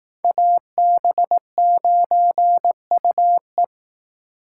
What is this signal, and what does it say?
Morse code, heard as a keyed tone, about 18 wpm, 700 Hz. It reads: AB9UE
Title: Speed Racing - USA 2x2 Call Signs 15wpm